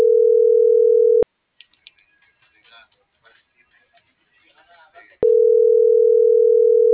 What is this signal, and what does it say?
Ringback tone (North America), 440 + 480 Hz.